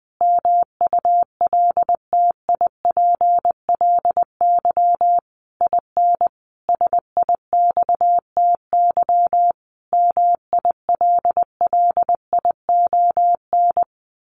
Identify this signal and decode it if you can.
Morse code, heard as a keyed tone, about 20 wpm, 700 Hz. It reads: MULTIPLY IN SIXTY MILLION
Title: A Sound Of Thunder 20wpm